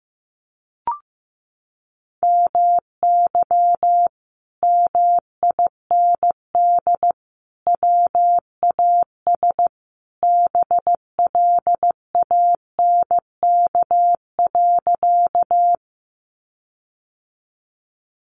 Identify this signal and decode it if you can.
Morse code, heard as a keyed tone, about 15 wpm, 700 Hz. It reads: MY MIND WAS BLANK.
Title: Sentences from Top 1500 Words 15wpm